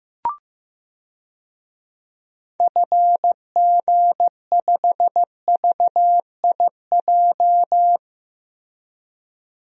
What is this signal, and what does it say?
Morse code, heard as a keyed tone, about 15 wpm, 700 Hz. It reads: FG5VIJ
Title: DXCC-100 Call Signs - 6 Characters - Repeated 2x - 15wpm